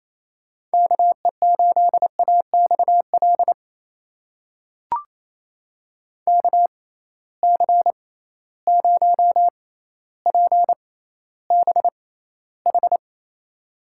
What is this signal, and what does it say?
Morse code, heard as a keyed tone, about 28 wpm, spaced as for 14 wpm, 700 Hz. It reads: KE8AXL KC0PBH
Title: USA Call Signs - 6 Characters - 6x Character Spacing - 28wpm